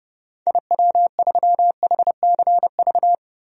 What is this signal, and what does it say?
Morse code, heard as a keyed tone, about 30 wpm, 700 Hz. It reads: IW3HCV
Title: DXCC-100 Call Signs - 6 Characters - Repeated 3x - 30wpm